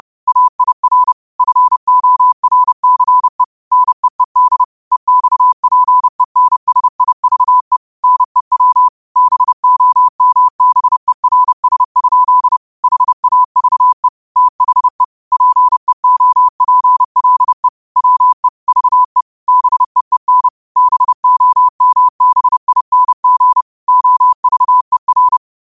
Morse code, heard as a keyed tone, about 30 wpm, 1000 Hz. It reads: AIR FORCE NEED EXPENSIVE NEW BOMBERS? HAVE THE PEOPLE WE VE BEEN BOMBING OVER
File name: qotdfiles / 30wpm